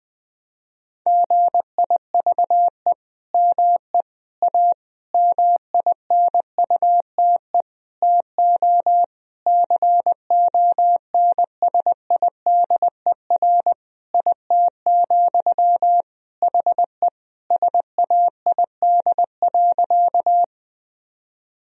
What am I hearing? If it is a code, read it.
Morse code, heard as a keyed tone, about 20 wpm, 700 Hz. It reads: GIVE ME A MINUTE TO CONSIDER IT, HE SAID.
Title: Sentences from Top 1000 Words - Review Part 4 - 20wpm